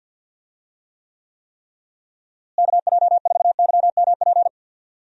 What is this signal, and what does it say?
Morse code, heard as a keyed tone, about 51 wpm, 700 Hz. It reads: KJ4YMP
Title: USA Call Signs - 6 Characters - Repeated 3x - 50wpm